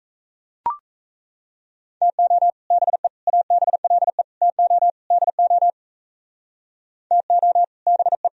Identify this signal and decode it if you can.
Morse code, heard as a keyed tone, about 37 wpm, 700 Hz. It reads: TO BE ABLE TO DO TO BE